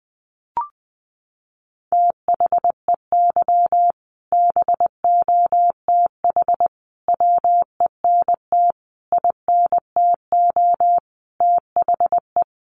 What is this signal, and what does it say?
Morse code, heard as a keyed tone, about 20 wpm, 700 Hz. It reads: THEY BOTH WENT INTO THE
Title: Sentences from Top 600 Words 20wpm